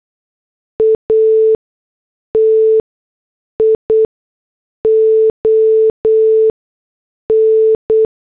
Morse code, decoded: ATION